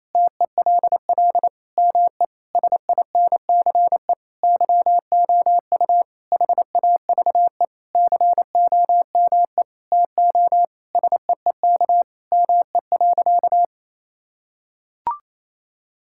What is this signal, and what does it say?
Morse code, heard as a keyed tone, about 28 wpm, 700 Hz. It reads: TELL ME SINCE YOU HAVE COME TO SEEK ME.
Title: Sentences from Top 1600 Words 28wpm